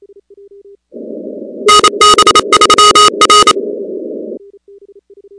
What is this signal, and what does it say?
Morse code, heard as a keyed tone, about 28 wpm, 1250 Hz, with noise. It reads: NB3R
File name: contests / 2019-cw-wpx-kt5j-k5pi- -2330 0-16